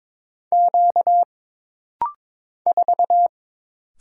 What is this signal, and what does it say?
Morse code, heard as a keyed tone, about 22 wpm, 700 Hz. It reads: Q 4
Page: This rapid-fire timing is geared to push your ICR (Instant Character Recognition) skills.